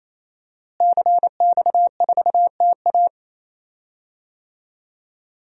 Morse code, heard as a keyed tone, about 28 wpm, 700 Hz. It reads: CX4TA
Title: DXCC-100 Call Signs - 5 Characters - Repeated 3x - 28wpm